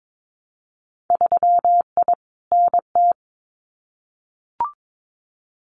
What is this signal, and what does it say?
Morse code, heard as a keyed tone, about 22 wpm, 700 Hz. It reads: 3I NT
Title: Winter Field Day Exchanges Encore 22wpm